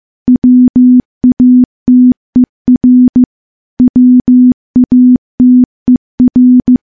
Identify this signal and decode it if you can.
Morse code, heard as a keyed tone, about 15 wpm, 250 Hz. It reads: WATER WATER
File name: morse-training-15WPM